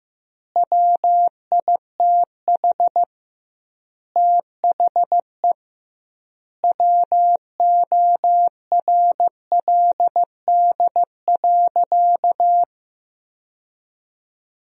Morse code, heard as a keyed tone, about 15 wpm, 700 Hz. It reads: WITH THE WORLD.
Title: 2x Word-Spacing - Sentences from Top 200 Words - 15wpm